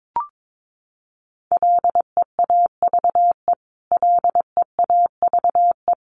Morse code, heard as a keyed tone, about 22 wpm, 700 Hz. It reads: LEAVE LEAVE
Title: Top 500 Words Repeated 22wpm